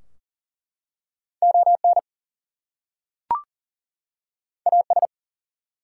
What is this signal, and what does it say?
Morse code, heard as a keyed tone, about 40 wpm, 700 Hz. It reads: ON AS
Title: Two Letter Words - 40wpm